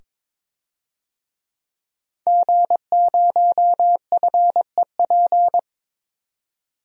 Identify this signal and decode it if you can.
Morse code, heard as a keyed tone, about 22 wpm, 700 Hz. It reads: G0FEP
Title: DXCC-100 Call Signs - 5 Characters - 22wpm